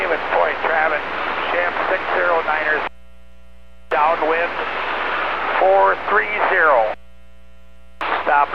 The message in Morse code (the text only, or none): none